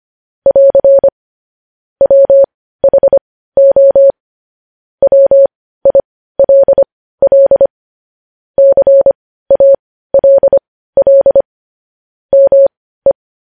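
Morse code, heard as a keyed tone, about 25 wpm, spaced as for 15 wpm, 550 Hz. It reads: <AR> WHO WILL CALL ME
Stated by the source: Morse code news headlines at 15 WPM.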